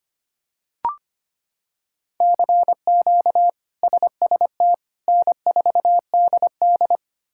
Morse code, heard as a keyed tone, about 25 wpm, 700 Hz. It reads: CQ SST N4DD
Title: K1USN SST QSO Practice - 25wpm